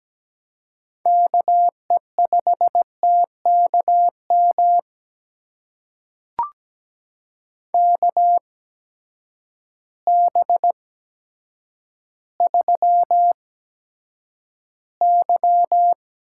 Morse code, heard as a keyed tone, about 17 wpm, spaced as for 9 wpm, 700 Hz. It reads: KE5TKM KB3Y
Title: USA Call Signs - 6 Characters - 8x Character Spacing - 17wpm